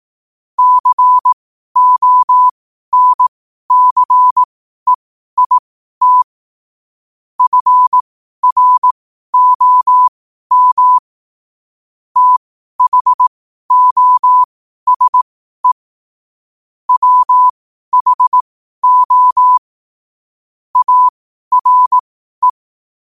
Morse code, heard as a keyed tone, about 18 wpm, spaced as for 12 wpm, 1000 Hz. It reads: CONCEIT FROM THOSE WHO ARE
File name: qotdfiles / 12wpm